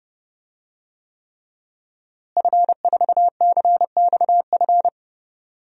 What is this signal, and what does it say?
Morse code, heard as a keyed tone, about 30 wpm, 700 Hz. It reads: F4CXF
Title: DXCC-100 Call Signs - 5 Characters - Repeated 3x - 30wpm